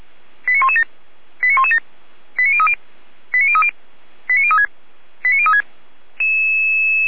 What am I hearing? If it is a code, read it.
Five-tone selective call (ZVEI), heard as quick runs of five tones, tones of about 70 ms. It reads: 89117 89117 89030 89030 89036 89036